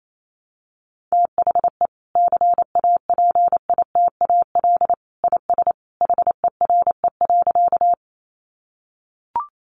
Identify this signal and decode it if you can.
Morse code, heard as a keyed tone, about 28 wpm, 700 Hz. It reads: THE CAPITAL IS HERE.
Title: Sentences from Top 1000 Words 28wpm